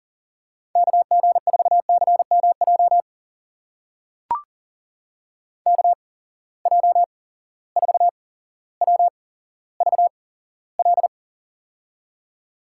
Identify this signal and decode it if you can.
Morse code, heard as a keyed tone, about 40 wpm, spaced as for 17 wpm, 700 Hz. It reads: KG4CMJ KJ4WVL
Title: USA Call Signs - 6 Characters - 8x Character Spacing - 40wpm